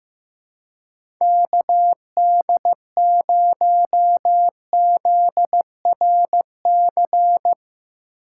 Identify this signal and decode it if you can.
Morse code, heard as a keyed tone, about 15 wpm, 700 Hz. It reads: KD0ZRC
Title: USA Call Signs - 6 Characters - 2x Character Spacing - 15wpm